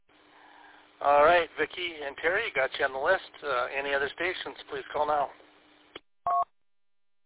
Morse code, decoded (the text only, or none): none